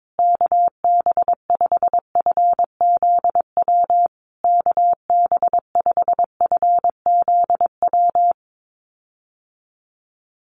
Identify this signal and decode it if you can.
Morse code, heard as a keyed tone, about 22 wpm, 700 Hz. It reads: KB5FZW KB5FZW
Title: USA Call Signs Repeated 3x 22wpm